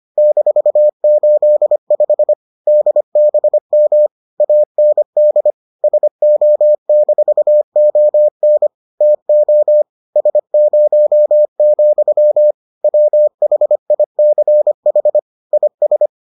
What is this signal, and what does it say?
Morse code, heard as a keyed tone, about 25 wpm, 600 Hz. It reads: -85 DBM AND SO-ON TO S0, WHICH IS